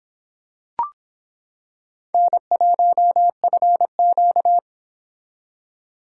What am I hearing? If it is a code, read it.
Morse code, heard as a keyed tone, about 26 wpm, 700 Hz. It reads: N1FQ